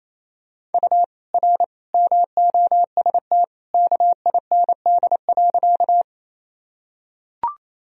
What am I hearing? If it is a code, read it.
Morse code, heard as a keyed tone, about 28 wpm, 700 Hz. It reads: U R MOST KIND.
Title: Sentences from Top 200 Words and Abbreviations 28wpm